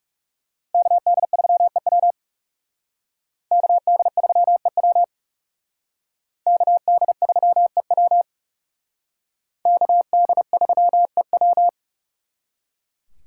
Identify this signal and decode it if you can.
Morse code, heard as a keyed tone, about 37 wpm, 700 Hz. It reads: KD3EW KD3EW KD3EW KD3EW
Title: Speed Racing - USA 2x2 Call Signs 30wpm